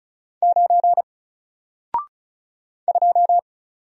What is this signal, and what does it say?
Morse code, heard as a keyed tone, about 36 wpm, 700 Hz. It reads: 9 2
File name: Single-Letter-Number-35wpm